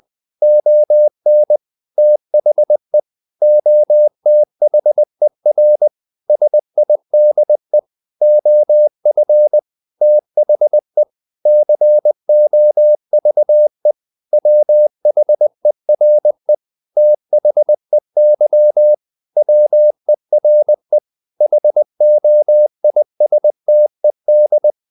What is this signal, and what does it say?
Morse code, heard as a keyed tone, about 20 wpm, 600 Hz. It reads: ON THE OTHER SIDE OF THE COVE WHERE THEY WERE HOISTED
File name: the old man and the sea / the old man and the sea part 1 20wpm